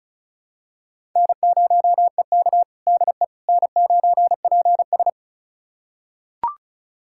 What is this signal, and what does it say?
Morse code, heard as a keyed tone, about 36 wpm, 700 Hz. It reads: N0EK DE N9PS
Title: USA Callsign DE Callsign - 4 Characters - 35wpm